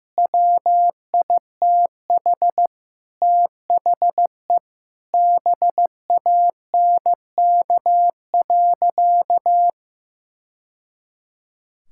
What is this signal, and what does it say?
Morse code, heard as a keyed tone, about 15 wpm, 700 Hz. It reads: WITH THE BANK.
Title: Sentences from Top 1000 Words - Review Part 6 - 15wpm